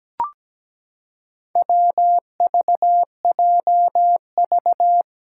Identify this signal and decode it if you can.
Morse code, heard as a keyed tone, about 17 wpm, 700 Hz. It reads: WVJV
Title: Four Challenging Letter-Number Combinations 17wpm